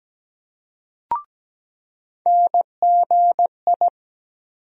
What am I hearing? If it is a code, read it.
Morse code, heard as a keyed tone, about 17 wpm, 700 Hz. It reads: NGI
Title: USA Call Sign Postfixes - 3 Letters 17wpm